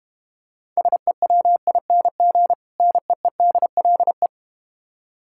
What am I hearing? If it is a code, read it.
Morse code, heard as a keyed tone, about 32 wpm, 700 Hz. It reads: SEWING NEEDLE